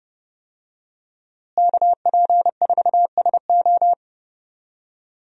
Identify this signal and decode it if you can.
Morse code, heard as a keyed tone, about 32 wpm, 700 Hz. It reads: KP4SO
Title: DXCC-100 Call Signs - 5 Characters - Repeated 2x - 30wpm